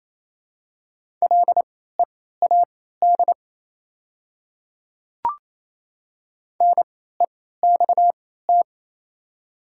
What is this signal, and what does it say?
Morse code, heard as a keyed tone, about 27 wpm, spaced as for 16 wpm, 700 Hz. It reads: LEAD NEXT
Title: 3x Character Spacing - Top 500 Words - 28wpm